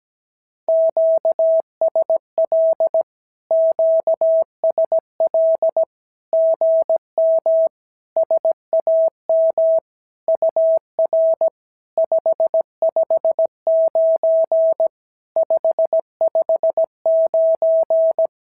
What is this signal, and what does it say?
Morse code, heard as a keyed tone, about 17 wpm, 690 Hz. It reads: QSL QSL GM SAM UR 559 559
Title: SOTA QSO Practice 17wpm